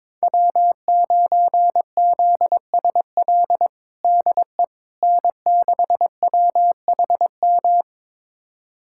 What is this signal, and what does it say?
Morse code, heard as a keyed tone, about 22 wpm, 700 Hz. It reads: W9ZSL DE N6WHM
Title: USA Callsign DE Callsign - 5 Characters - 22wpm